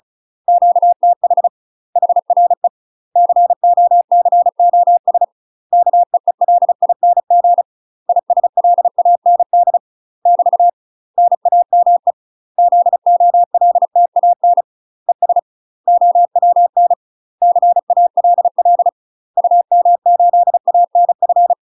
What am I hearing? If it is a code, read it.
Morse code, heard as a keyed tone, about 34 wpm, 700 Hz. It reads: QTH HRE COCOS KEELING ISLAND <BT> NAME ZOLTAN ES OWN CALL UM8ANF
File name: UK MASTER QSOs TRAINING / 35WPM / UK MASTER QSOs 0000